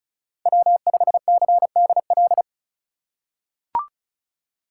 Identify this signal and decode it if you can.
Morse code, heard as a keyed tone, about 36 wpm, 700 Hz. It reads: W5CDL
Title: USA Call Signs - 5 Characters - 2x Character Spacing - 35wpm